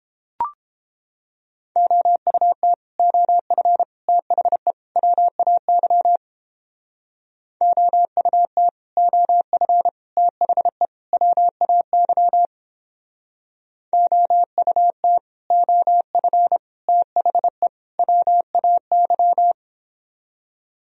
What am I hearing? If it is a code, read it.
Morse code, heard as a keyed tone, about 29 wpm, 700 Hz. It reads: OUT OF THE WAY OUT OF THE WAY OUT OF THE WAY